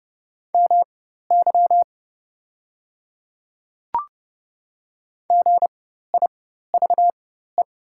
Morse code, heard as a keyed tone, about 30 wpm, spaced as for 14 wpm, 690 Hz. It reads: MY GIVE
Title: 4x Character Spacing - Top 500 Words - 30wpm